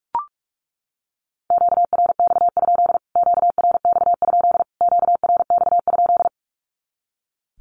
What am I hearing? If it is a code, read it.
Morse code, heard as a keyed tone, about 44 wpm, 700 Hz. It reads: QRX? QRX? QRX?
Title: Q-Codes Repeated 3x 45wpm